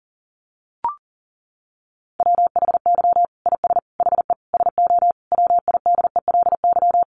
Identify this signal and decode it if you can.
Morse code, heard as a keyed tone, about 40 wpm, 700 Hz. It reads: WHY IS HE SO WIDELY
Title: Sentences from Top 1200 Words 40wpm